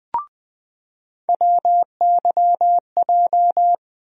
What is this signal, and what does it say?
Morse code, heard as a keyed tone, about 20 wpm, 700 Hz. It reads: WYJ
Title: USA Call Sign Postfixes - 3 Letters 20wpm